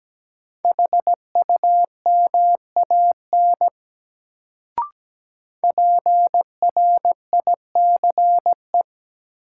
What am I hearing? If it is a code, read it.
Morse code, heard as a keyed tone, about 17 wpm, 700 Hz. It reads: HUMAN PRICE
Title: Sentences from Top 600 Words - New Word Drill - 17wpm